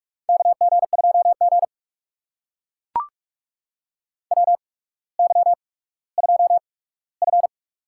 Morse code, heard as a keyed tone, about 44 wpm, spaced as for 19 wpm, 700 Hz. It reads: KG2G WY2F
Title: USA Call Signs - 4 Characters - 8x Character Spacing - 45wpm